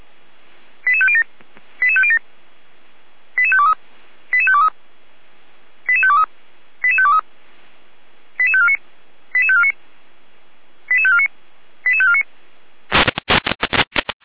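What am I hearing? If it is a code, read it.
Five-tone selective call (ZVEI), heard as quick runs of five tones, tones of about 80 ms. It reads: 88507 88507 88523 88523 88523 88523 88549 88549 88549 88549
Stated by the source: Five-tone ZVEI call received on an SDR.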